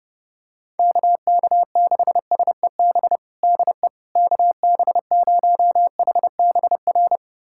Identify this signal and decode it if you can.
Morse code, heard as a keyed tone, about 30 wpm, 700 Hz. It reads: KK6SEB DE KB0HBR
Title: USA Callsign DE Callsign - 6 Characters - 30wpm